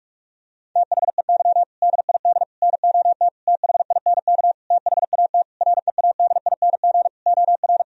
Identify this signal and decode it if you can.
Morse code, heard as a keyed tone, about 46 wpm, 700 Hz. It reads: THEY DID NOT THINK THAT READING OR
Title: Sentences from Top 1000 Words - Review Part 3 - 45wpm